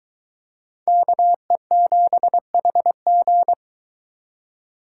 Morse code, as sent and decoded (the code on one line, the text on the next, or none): -.- . --... .... --.
KE7HG